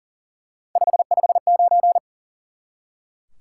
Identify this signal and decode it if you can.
Morse code, heard as a keyed tone, about 40 wpm, 700 Hz. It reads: 559
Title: Top 90 QSO Elements 40wpm